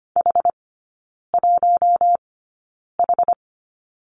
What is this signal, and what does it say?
Morse code, heard as a keyed tone, about 25 wpm, spaced as for 9 wpm, 700 Hz. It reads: H1H
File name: course / 25:10wpm / 093 Lesson 16e - Letter c - Three Characters 25wpm